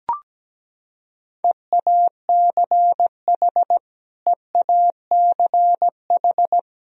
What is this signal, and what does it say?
Morse code, heard as a keyed tone, about 17 wpm, 700 Hz. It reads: EACH EACH
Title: Top 100 Words Repeated 17wpm